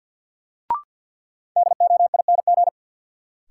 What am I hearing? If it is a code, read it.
Morse code, heard as a keyed tone, about 52 wpm, 700 Hz. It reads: DOING
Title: Sets of 4 Words - New Word Drill - 50wpm